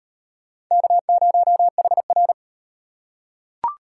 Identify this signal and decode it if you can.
Morse code, heard as a keyed tone, about 36 wpm, 700 Hz. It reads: K0HR